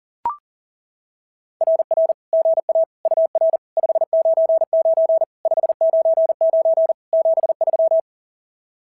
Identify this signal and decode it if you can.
Morse code, heard as a keyed tone, about 41 wpm, 690 Hz. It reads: RR GA UR 599 599 73
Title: SOTA QSO Practice 40wpm